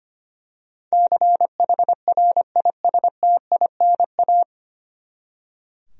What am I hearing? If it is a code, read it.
Morse code, heard as a keyed tone, about 25 wpm, 700 Hz. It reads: CHRISTINA
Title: USA Common Names - 25wpm